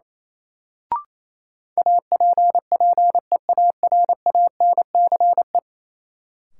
Morse code, heard as a keyed tone, about 28 wpm, 700 Hz. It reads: APPEARANCE
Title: Sentences from Top 1100 Words - New Word Drill Alt - 28wpm